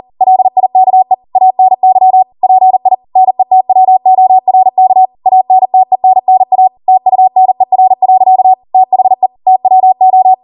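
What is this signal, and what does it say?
Morse code, heard as a keyed tone, about 40 wpm, 750 Hz. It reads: LIKE ANY PI NETWORK ANTENNA TUNER. THE TWO
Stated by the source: Code practice at 40 wpm.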